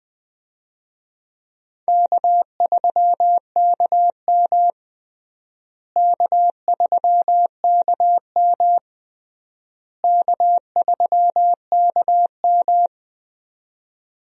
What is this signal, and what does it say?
Morse code, heard as a keyed tone, about 21 wpm, 700 Hz. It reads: K3KM K3KM K3KM
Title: USA Call Signs - 4 Characters - Repeated 3x - 20wpm